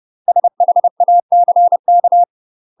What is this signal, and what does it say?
Morse code, heard as a keyed tone, about 30 wpm, 700 Hz. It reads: SHACK
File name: UK MASTER QSOs TRAINING / 30WPM / UK MASTER QSOs 0016